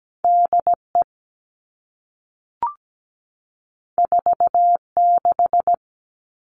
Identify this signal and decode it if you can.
Morse code, heard as a keyed tone, about 17 wpm, 700 Hz. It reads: DE 46
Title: All Cut Numbers - 2-Digit 17wpm